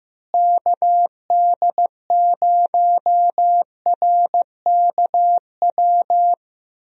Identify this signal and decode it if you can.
Morse code, heard as a keyed tone, about 15 wpm, 700 Hz. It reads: KD0RKW
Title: USA Call Signs - 6 Characters - Repeated 3x - 15wpm